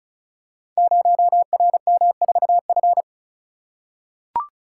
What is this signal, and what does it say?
Morse code, heard as a keyed tone, about 36 wpm, 700 Hz. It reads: 0RM4F
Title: Code Groups - 5 Letter:Numbers - 35wpm